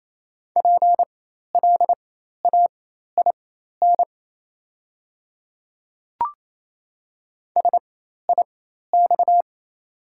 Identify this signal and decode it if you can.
Morse code, heard as a keyed tone, about 28 wpm, spaced as for 13 wpm, 700 Hz. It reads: PLAIN SIX
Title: 4x Character Spacing - Top 500 Words - 28wpm